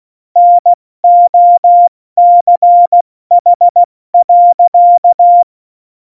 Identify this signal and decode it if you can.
Morse code, heard as a keyed tone, about 16 wpm, 700 Hz. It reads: NOCH.
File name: Morsekurs / Witze / Witz 08. Dezember 2020, 70 F80